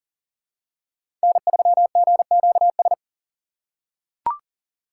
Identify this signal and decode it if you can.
Morse code, heard as a keyed tone, about 40 wpm, 700 Hz. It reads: N3GQS
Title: USA Call Signs - 5 Characters - 3x Character Spacing - 40wpm